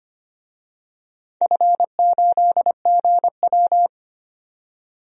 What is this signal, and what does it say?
Morse code, heard as a keyed tone, about 25 wpm, 700 Hz. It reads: F8GW
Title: DXCC-100 Call Signs - 4 Characters - 25wpm